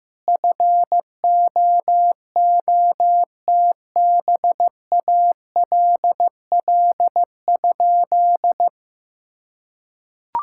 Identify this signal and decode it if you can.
Morse code, heard as a keyed tone, about 15 wpm, 700 Hz. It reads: FOOTBALL?
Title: Sentences from Top 1600 Words 15wpm